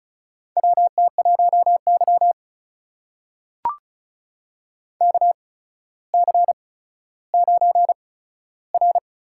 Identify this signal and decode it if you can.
Morse code, heard as a keyed tone, about 36 wpm, spaced as for 15 wpm, 700 Hz. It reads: WT1Y KC9R
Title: USA Call Signs - 8x Character Spacing - 35wpm